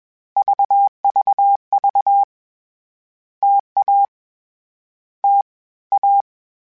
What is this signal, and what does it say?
Morse code, heard as a keyed tone, about 20 wpm, 800 Hz. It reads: VVV TA T A
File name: Difficult Combos / Difficult Combos-mp3s / T AOEI @20